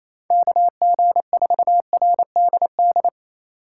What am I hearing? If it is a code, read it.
Morse code, heard as a keyed tone, about 27 wpm, 700 Hz. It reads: KG4RDD
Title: USA Call Signs - 6 Characters - 4x Character Spacing - 28wpm